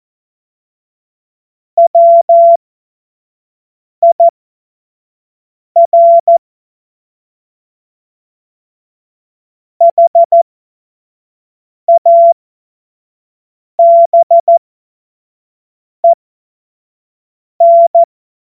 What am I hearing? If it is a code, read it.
Morse code, heard as a keyed tone, about 14 wpm, spaced as for 5 wpm, 680 Hz. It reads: WIR HABEN
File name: Morsekurs / Witze / Witz 14. März 2023 25 F70